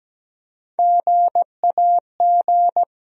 Morse code, heard as a keyed tone, about 17 wpm, 700 Hz. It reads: GAG
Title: USA Call Sign Postfixes - 3 Letters 17wpm